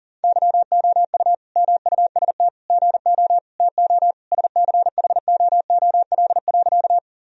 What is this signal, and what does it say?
Morse code, heard as a keyed tone, about 40 wpm, 700 Hz. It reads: YOU MUST GO TO SCHOOL.
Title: Sentences from Top 1000 Words - Review Part 1 - 40wpm